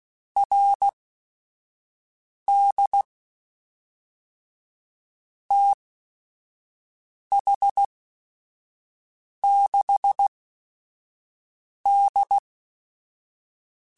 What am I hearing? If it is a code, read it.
Morse code, heard as a keyed tone, about 16 wpm, spaced as for 5 wpm, 800 Hz. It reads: RD TH6D
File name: QSOs / QSOs-mp3s / WD0BGP 1@5